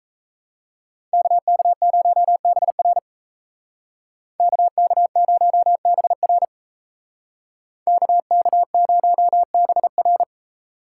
Morse code, heard as a keyed tone, about 36 wpm, 700 Hz. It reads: KK0BR KK0BR KK0BR